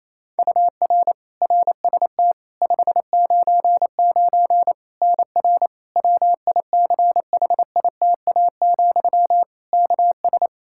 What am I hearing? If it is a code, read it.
Morse code, heard as a keyed tone, about 28 wpm, 700 Hz. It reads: UR RST 599 NR WICHITA, KS
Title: K1USN SST QSO Practice 28wpm